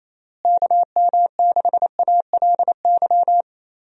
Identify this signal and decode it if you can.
Morse code, heard as a keyed tone, about 28 wpm, 700 Hz. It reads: KM6ALY
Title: USA Call Signs - 6 Characters - Repeated 2x - 28wpm